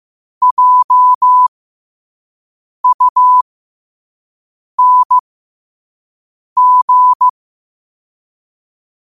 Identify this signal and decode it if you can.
Morse code, heard as a keyed tone, about 15 wpm, spaced as for 5 wpm, 1000 Hz. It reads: JUNG
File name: 5wpm